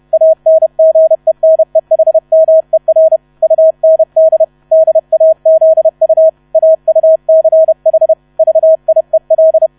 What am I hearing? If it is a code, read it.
Morse code, heard as a keyed tone, about 31 wpm, 640 Hz. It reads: ANGENEHMER UND DAZU AUCH VIEL
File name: 30wpm